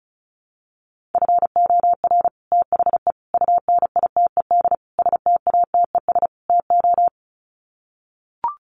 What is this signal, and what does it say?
Morse code, heard as a keyed tone, about 34 wpm, 700 Hz. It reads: FOR THE UNITED STATES TO
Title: Sets of 5 Words Encore 35wpm